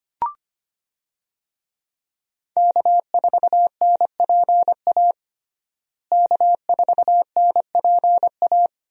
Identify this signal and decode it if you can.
Morse code, heard as a keyed tone, about 25 wpm, 700 Hz. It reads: K4NPA K4NPA
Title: USA Call Signs - 5 Characters - Repeated 3x - 25wpm